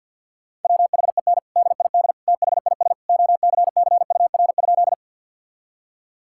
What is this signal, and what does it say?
Morse code, heard as a keyed tone, about 50 wpm, 700 Hz. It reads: WHEN DID THIS OCCUR?
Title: Sentences from Top 1000 Words - Review Part 6 - 50wpm